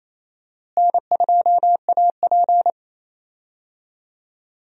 Morse code, heard as a keyed tone, about 28 wpm, 700 Hz. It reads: N2AP
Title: USA Call Signs - 4 Characters - Repeated 3x - 28wpm